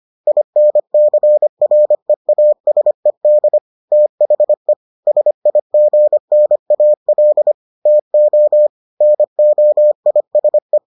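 Morse code, heard as a keyed tone, about 25 wpm, 600 Hz. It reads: INCREASED THE SIGNAL TO NOISE
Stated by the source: Morse edition of the Foundations of Amateur Radio episodes.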